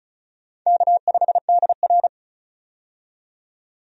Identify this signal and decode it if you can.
Morse code, heard as a keyed tone, about 36 wpm, 700 Hz. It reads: K5DR
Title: USA Call Signs - 4 Characters - 35wpm